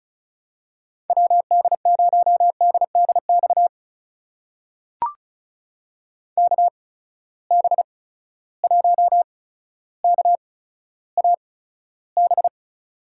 Morse code, heard as a keyed tone, about 36 wpm, spaced as for 15 wpm, 700 Hz. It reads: WD0DDX KB1KAB
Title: USA Call Signs - 6 Characters - 8x Character Spacing - 35wpm